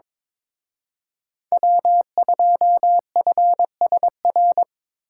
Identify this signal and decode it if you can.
Morse code, heard as a keyed tone, about 25 wpm, 700 Hz. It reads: W2FSR
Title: Speed Racing - USA Call Signs 22wpm